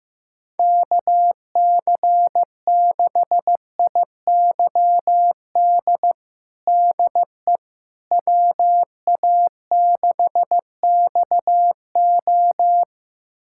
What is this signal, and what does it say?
Morse code, heard as a keyed tone, about 16 wpm, 700 Hz. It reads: KC6IYD DE WA6XO
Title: USA Callsign DE Callsign 15wpm